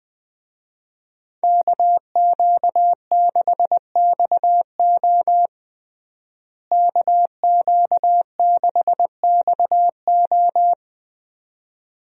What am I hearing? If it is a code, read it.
Morse code, heard as a keyed tone, about 20 wpm, 700 Hz. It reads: KQ6XO KQ6XO
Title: USA Call Signs - 5 Characters - Repeated 2x - 20wpm